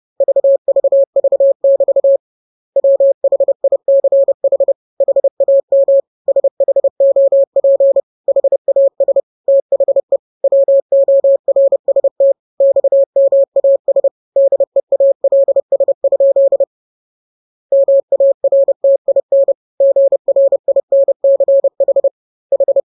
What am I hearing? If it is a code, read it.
Morse code, heard as a keyed tone, about 30 wpm, 550 Hz. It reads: VVV<BT> WHICH HAM SHOP HAS THE WORST XMAS DEALS? MARTIN GRINCH H
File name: jokes / joke 7 30 0000